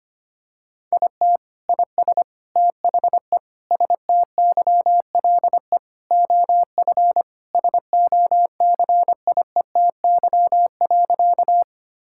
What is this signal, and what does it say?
Morse code, heard as a keyed tone, about 25 wpm, 700 Hz. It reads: IT IS THE STYLE OF SOCIETY.
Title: Sentences from Top 1000 Words - Review Part 5 - 25wpm